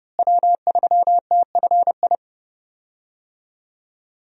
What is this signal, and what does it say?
Morse code, heard as a keyed tone, about 32 wpm, 700 Hz. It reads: W3TFI
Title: USA Call Signs - 5 Characters - Repeated 3x - 30wpm